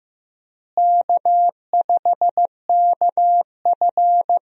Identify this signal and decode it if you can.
Morse code, heard as a keyed tone, about 15 wpm, 700 Hz. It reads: K5KF
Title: USA Call Signs - 4 Characters - Repeated 3x - 15wpm